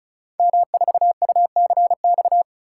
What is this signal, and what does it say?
Morse code, heard as a keyed tone, about 36 wpm, 690 Hz. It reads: M4UCX
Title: Code Groups - 5 Letter:Numbers - 35wpm